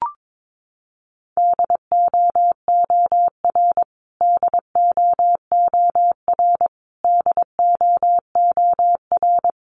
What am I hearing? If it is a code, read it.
Morse code, heard as a keyed tone, about 22 wpm, 700 Hz. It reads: DOOR DOOR DOOR
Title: Top 500 Words Repeated 22wpm